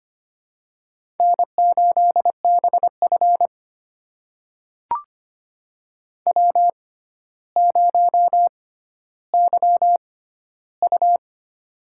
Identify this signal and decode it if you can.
Morse code, heard as a keyed tone, about 25 wpm, spaced as for 13 wpm, 700 Hz. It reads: N8BF W0YU
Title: USA Call Signs - 4 Characters - 6x Character Spacing - 25wpm